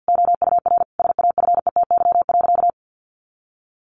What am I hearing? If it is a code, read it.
Morse code, heard as a keyed tone, about 48 wpm, 700 Hz. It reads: OUR SAFETY.
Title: Sentences from Top 700 Words 50wpm